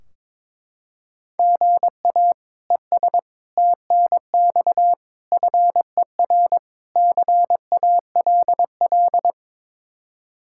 Morse code, heard as a keyed tone, about 22 wpm, 700 Hz. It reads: GA ES TNX FER CALL
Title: Practice QSO - Exchange 1 - Beginning - 22wpm